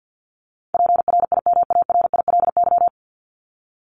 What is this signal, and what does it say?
Morse code, heard as a keyed tone, about 50 wpm, 700 Hz. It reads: PRIMARILY
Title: LY Suffix - Top 100 Words with Base Words 50wpm